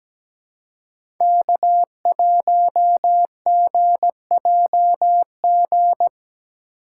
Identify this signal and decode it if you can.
Morse code, heard as a keyed tone, about 17 wpm, 700 Hz. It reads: K1GJG
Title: USA Call Signs - 5 Characters - 2x Character Spacing - 17wpm